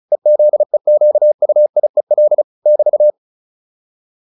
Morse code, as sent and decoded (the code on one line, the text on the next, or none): . --.. --.- ..- .. .-.. / -...-
EZEQUIEL <BT>